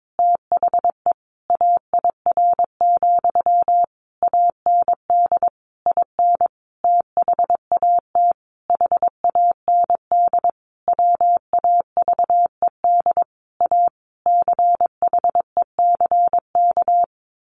Morse code, heard as a keyed tone, about 22 wpm, 700 Hz. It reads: THE AIR, AND IN THAT HAND WAVED A CHECK
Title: A Sound Of Thunder 22wpm